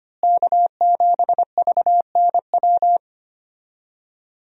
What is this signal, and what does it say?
Morse code, heard as a keyed tone, about 25 wpm, 700 Hz. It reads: K7VNW
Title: USA Call Signs - 5 Characters - Repeated 2x - 25wpm